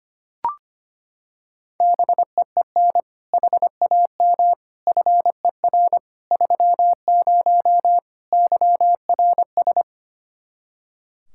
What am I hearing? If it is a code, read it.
Morse code, heard as a keyed tone, about 25 wpm, 700 Hz. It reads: BEEN HAM FER 30 YRS
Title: K1USN SST QSO Practice 25wpm